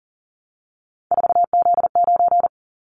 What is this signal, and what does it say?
Morse code, heard as a keyed tone, about 40 wpm, 700 Hz. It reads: 4Z9
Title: Callsign Prefixes for Top 50 DXCC - 40wpm